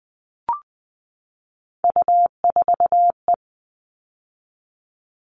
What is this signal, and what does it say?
Morse code, heard as a keyed tone, about 20 wpm, 700 Hz. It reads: U4E
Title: All Cut Numbers - 3-Digit 20wpm